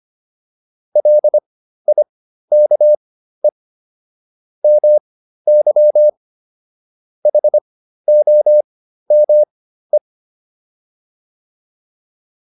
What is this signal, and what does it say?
Morse code, heard as a keyed tone, about 25 wpm, spaced as for 13 wpm, 600 Hz. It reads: LIKE MY HOME